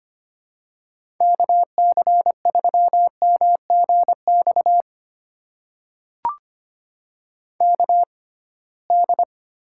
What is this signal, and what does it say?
Morse code, heard as a keyed tone, about 25 wpm, spaced as for 19 wpm, 700 Hz. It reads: KC3MGX KD
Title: USA Call Signs - 6 Characters - 6x Character Spacing - 25wpm